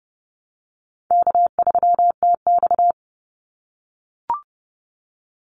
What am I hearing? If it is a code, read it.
Morse code, heard as a keyed tone, about 30 wpm, 700 Hz. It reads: K3TX
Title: Call Signs - 30wpm